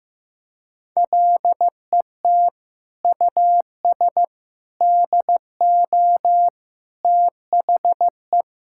Morse code, heard as a keyed tone, about 15 wpm, 700 Hz. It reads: LET US DO THE
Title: Sentences from Top 1600 Words 15wpm